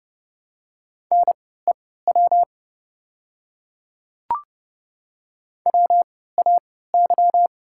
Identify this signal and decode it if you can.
Morse code, heard as a keyed tone, about 30 wpm, spaced as for 17 wpm, 690 Hz. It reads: NEW WAY
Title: 3x Character Spacing - Top 300 Words - 30wpm